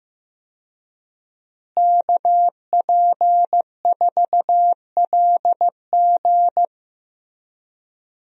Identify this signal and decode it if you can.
Morse code, heard as a keyed tone, about 15 wpm, 700 Hz. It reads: KP4LG
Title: DXCC-100 Call Signs - 5 Characters - Repeated 2x - 15wpm